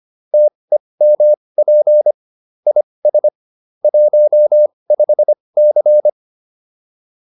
Morse code, heard as a keyed tone, about 25 wpm, spaced as for 20 wpm, 600 Hz. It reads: TEMP IS 15C